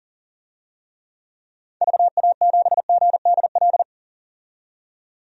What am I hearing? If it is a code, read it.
Morse code, heard as a keyed tone, about 40 wpm, 700 Hz. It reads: VA7GDL
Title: DXCC-100 Call Signs - 6 Characters - 40wpm